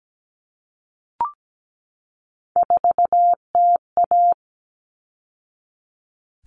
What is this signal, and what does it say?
Morse code, heard as a keyed tone, about 17 wpm, 700 Hz. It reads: 4TA
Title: All Cut Numbers - 3-Digit 17wpm